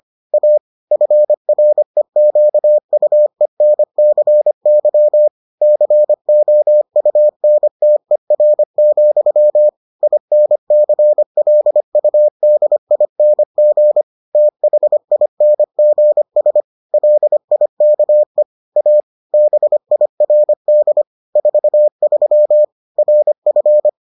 Morse code, heard as a keyed tone, about 25 wpm, 600 Hz. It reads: A FREQUENCY COUNTER, INCLUDING THINGS LIKE A BIRD 43 RF